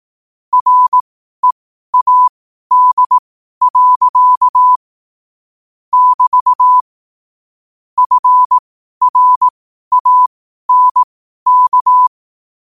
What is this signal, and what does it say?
Morse code, heard as a keyed tone, about 18 wpm, spaced as for 12 wpm, 1000 Hz. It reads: READ. <BT> FRANK
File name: qotdfiles / 12wpm